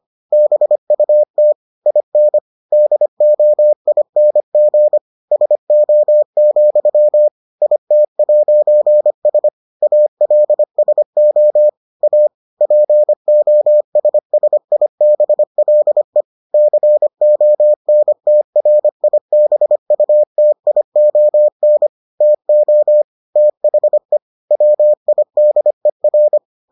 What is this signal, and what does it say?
Morse code, heard as a keyed tone, about 25 wpm, 600 Hz. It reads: BUT IN DOING SO, IT'S ALSO A POSSIBLE CONTRIBUTION TO THE WIDER